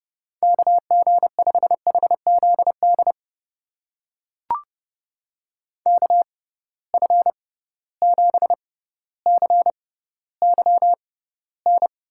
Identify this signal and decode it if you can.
Morse code, heard as a keyed tone, about 31 wpm, spaced as for 16 wpm, 700 Hz. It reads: KG5HZD KF7CYN
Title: USA Call Signs - 6 Characters - 6x Character Spacing - 30wpm